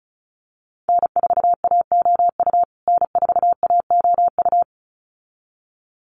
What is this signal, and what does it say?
Morse code, heard as a keyed tone, about 34 wpm, 700 Hz. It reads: N4AOU N4AOU
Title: USA Call Signs Repeated 2x 35wpm